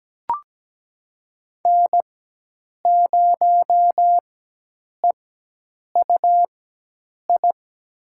Morse code, heard as a keyed tone, about 17 wpm, spaced as for 8 wpm, 700 Hz. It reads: N0EUI
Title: USA Call Signs - 5 Characters - 4x Character Spacing - 17wpm